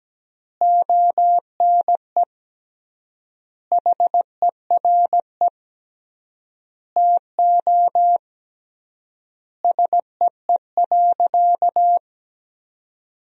Morse code, heard as a keyed tone, about 17 wpm, 700 Hz. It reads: ONE HERE TO SEE.
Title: 3x Word-Spacing - Sentences from Top 200 Words - 17wpm